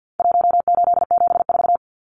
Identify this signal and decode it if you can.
Morse code, heard as a keyed tone, about 50 wpm, 700 Hz. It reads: 1874
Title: Numbers - Contest Style - 4-Digit 50wpm